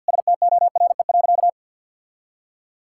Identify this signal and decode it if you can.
Morse code, heard as a keyed tone, about 52 wpm, 690 Hz. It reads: STORE.
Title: Sentences from Top 1000 Words - Review Part 5 - 50wpm